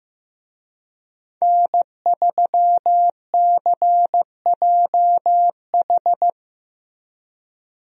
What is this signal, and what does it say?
Morse code, heard as a keyed tone, about 16 wpm, 700 Hz. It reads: N3CJH
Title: USA Call Signs - 5 Characters - Repeated 2x - 15wpm